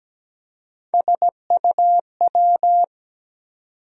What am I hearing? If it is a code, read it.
Morse code, heard as a keyed tone, about 17 wpm, 700 Hz. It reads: SUW
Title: USA Call Sign Postfixes - 3 Letters 17wpm